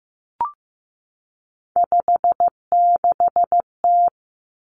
Morse code, heard as a keyed tone, about 15 wpm, 700 Hz. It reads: 56T
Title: Three Challenging Letter-Number Combinations 15wpm